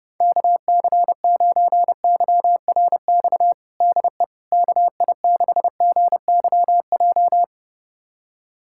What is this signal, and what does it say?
Morse code, heard as a keyed tone, about 30 wpm, 700 Hz. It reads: KC9YRX DE KI6GYJ
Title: USA Callsign DE Callsign 30wpm